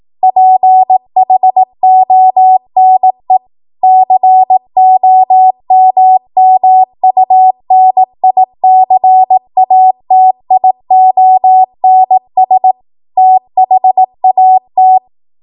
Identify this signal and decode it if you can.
Morse code, heard as a keyed tone, about 18 wpm, 750 Hz. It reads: PHONE COMMUNICATIONS THAT
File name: Morse / 18WPM